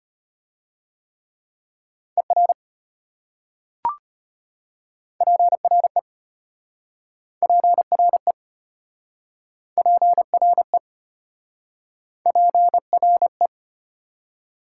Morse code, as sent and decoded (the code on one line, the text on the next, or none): . .-. / .--. .-. / .--. .-. / .--. .-. / .--. .-. .
ER PRE PRE PRE PRE